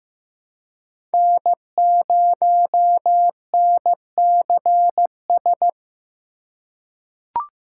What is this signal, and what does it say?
Morse code, heard as a keyed tone, about 15 wpm, 700 Hz. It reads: N0NCS
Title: USA Call Signs - 5 Characters - 4x Character Spacing - 15wpm